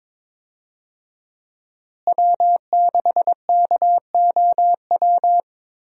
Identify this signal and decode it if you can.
Morse code, heard as a keyed tone, about 22 wpm, 700 Hz. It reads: W6KOW
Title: USA Call Signs - 5 Characters - Repeated 2x - 22wpm